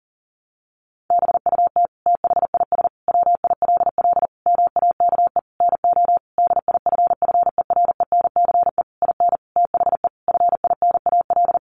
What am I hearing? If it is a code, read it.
Morse code, heard as a keyed tone, about 37 wpm, 700 Hz. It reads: BUT THIS WILL MAKE NO DIFFERENCE IN THE FINAL
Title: Sentences from Top 800 Words 40wpm